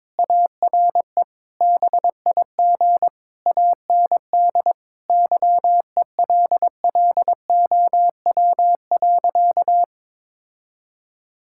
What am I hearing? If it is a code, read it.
Morse code, heard as a keyed tone, about 21 wpm, 700 Hz. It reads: ARE BIG AND YELLOW.
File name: practice / 22wpm / Easy Sentences with Hints - 22wpm